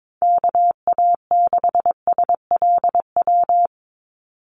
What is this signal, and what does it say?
Morse code, heard as a keyed tone, about 22 wpm, 700 Hz. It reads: KA6SLW
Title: USA Call Signs 22wpm